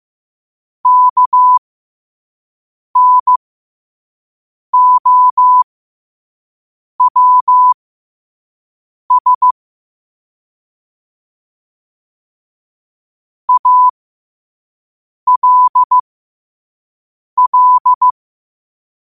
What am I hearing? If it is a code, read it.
Morse code, heard as a keyed tone, about 15 wpm, spaced as for 5 wpm, 1000 Hz. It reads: KNOWS ALL
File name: qotdfiles / 5wpm